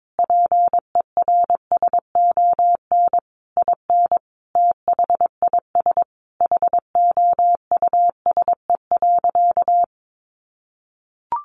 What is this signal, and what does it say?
Morse code, heard as a keyed tone, about 22 wpm, 700 Hz. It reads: PERSON IN THIS HOUSE.
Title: Sentences from Top 400 Words 22wpm